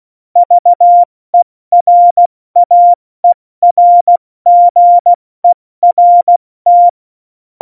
Morse code, heard as a keyed tone, about 16 wpm, 700 Hz. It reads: VERAERGERT
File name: Morsekurs / Witze / Witz 14. August 2018, 70 F80